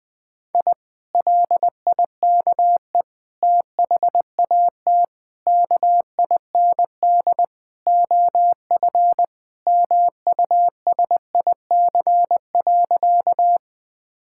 Morse code, heard as a keyed tone, about 20 wpm, 700 Hz. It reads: I LIKE THAT KIND OF MUSIC.
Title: Sentences with 3-Word Phrases - Spoken Beginning - 20wpm